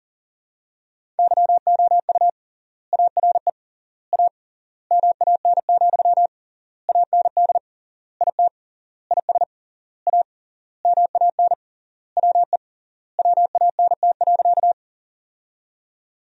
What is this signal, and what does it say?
Morse code, heard as a keyed tone, about 40 wpm, 700 Hz. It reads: YOU ARE A MAN, AND IT IS A MAN WE WANT.
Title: 3x Word-Spacing - Sentences from Top 200 Words - 40wpm